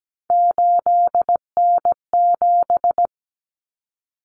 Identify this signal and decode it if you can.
Morse code, heard as a keyed tone, about 17 wpm, 700 Hz. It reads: 8N7
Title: Common Cut Numbers - 3-Digit 17wpm